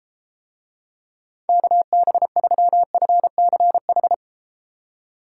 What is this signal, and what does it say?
Morse code, heard as a keyed tone, about 30 wpm, 700 Hz. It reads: KB3FCH